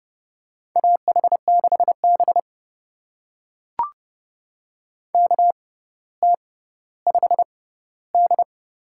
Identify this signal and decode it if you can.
Morse code, heard as a keyed tone, about 30 wpm, spaced as for 15 wpm, 700 Hz. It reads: AH6B KT5D
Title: USA Call Signs - 6x Character Spacing - 30wpm